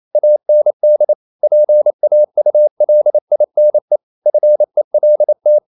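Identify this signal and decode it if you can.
Morse code, heard as a keyed tone, about 28 wpm, 600 Hz. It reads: AND PAULINE FELT
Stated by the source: Morse code practice at 28 wpm, text of Ken Follett's novel Never in English, from CW med Gnister.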